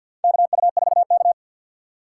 Morse code, heard as a keyed tone, about 52 wpm, 700 Hz. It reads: KU4K
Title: USA Call Signs - 3x Character Spacing - 50wpm